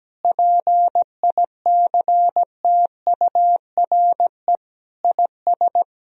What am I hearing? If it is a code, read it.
Morse code, heard as a keyed tone, about 17 wpm, 700 Hz. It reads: PICTURE IS
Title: Sentences from Top 1000 Words - Review Part 1 - 17wpm